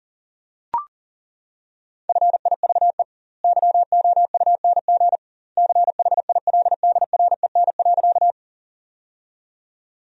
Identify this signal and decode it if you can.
Morse code, heard as a keyed tone, about 40 wpm, 700 Hz. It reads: FIVE YOUNG CHILDREN.
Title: 2x Word-Spacing - Sentences from Top 400 Words - 40wpm